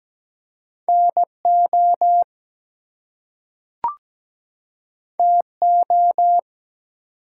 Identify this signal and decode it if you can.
Morse code, heard as a keyed tone, about 17 wpm, 700 Hz. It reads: NO TO
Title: Two Letter Words - 17wpm